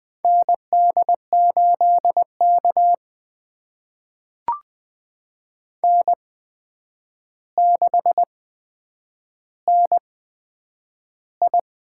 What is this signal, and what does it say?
Morse code, heard as a keyed tone, about 20 wpm, spaced as for 9 wpm, 700 Hz. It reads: ND8K N6NI
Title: USA Call Signs - 4 Characters - 8x Character Spacing - 20wpm